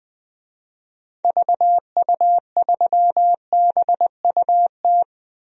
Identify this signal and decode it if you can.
Morse code, heard as a keyed tone, about 21 wpm, 700 Hz. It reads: VU3BUT
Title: DXCC-100 Call Signs - 6 Characters - 20wpm